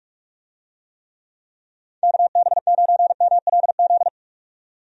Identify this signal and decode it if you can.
Morse code, heard as a keyed tone, about 43 wpm, 700 Hz. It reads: KB9MLZ
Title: USA Call Signs - 6 Characters - 45wpm